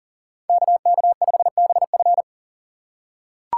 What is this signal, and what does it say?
Morse code, heard as a keyed tone, about 40 wpm, 700 Hz. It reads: KK5BF
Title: USA Call Signs - 5 Characters - 2x Character Spacing - 40wpm